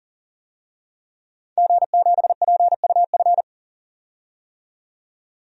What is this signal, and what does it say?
Morse code, heard as a keyed tone, about 40 wpm, 700 Hz. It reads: G7PUF
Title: DXCC-100 Call Signs - 5 Characters - 40wpm